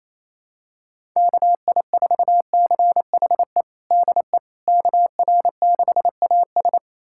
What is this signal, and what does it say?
Morse code, heard as a keyed tone, about 27 wpm, 700 Hz. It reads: KI4CHE DE KR6AS
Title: USA Callsign DE Callsign 28wpm